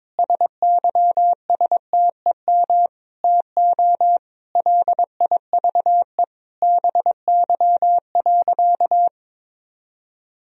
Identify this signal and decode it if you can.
Morse code, heard as a keyed tone, about 21 wpm, 700 Hz. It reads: SYSTEM TO LIVE BY.
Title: Sentences from Top 1000 Words - Review Part 3 - 22wpm